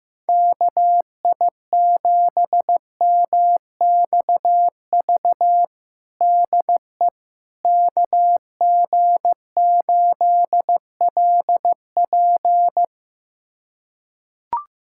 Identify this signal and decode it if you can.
Morse code, heard as a keyed tone, about 15 wpm, 700 Hz. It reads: KI7MXV DE KG8LP
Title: USA Callsign DE Callsign 15wpm